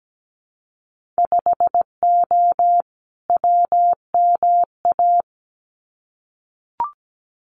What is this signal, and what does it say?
Morse code, heard as a keyed tone, about 17 wpm, 700 Hz. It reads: 5O WMA
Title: Winter Field Day Exchanges Encore 17wpm